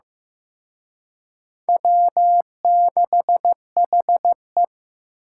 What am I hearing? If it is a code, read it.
Morse code, heard as a keyed tone, about 18 wpm, 700 Hz. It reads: W6HE
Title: Speed Racing - USA 1x2 Call Signs 15wpm